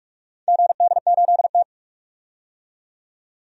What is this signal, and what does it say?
Morse code, heard as a keyed tone, about 46 wpm, 700 Hz. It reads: GD8T
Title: DXCC-100 Call Signs - 4 Characters - 45wpm